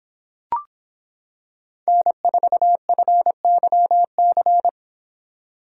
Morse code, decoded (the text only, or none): N4FYC